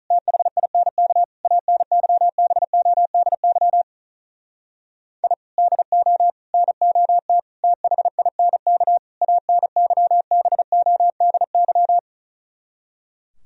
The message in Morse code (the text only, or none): THINK ANYBODY I DO NOT THINK ANYBODY